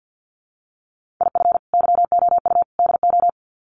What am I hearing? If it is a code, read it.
Morse code, heard as a keyed tone, about 48 wpm, 700 Hz. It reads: IF YOU DO
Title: Sets of 3 Words 50wpm